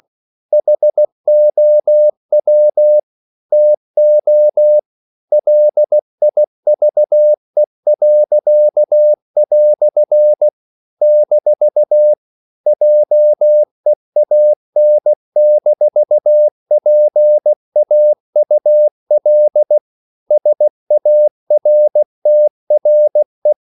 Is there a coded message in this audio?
Yes: Morse code from 0.5 to 23.5 s, heard as a keyed tone, about 16 wpm, 600 Hz. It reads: HOW TO LIVE." - JEAN-PAUL SARTRE